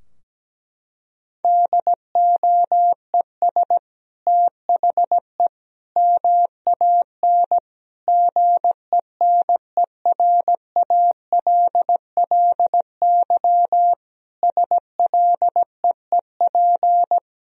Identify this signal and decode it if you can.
Morse code, heard as a keyed tone, about 17 wpm, 700 Hz. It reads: DOES THE MAN GENERALLY SLEEP
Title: Sentences from Top 1000 Words - Review Part 4 - 17wpm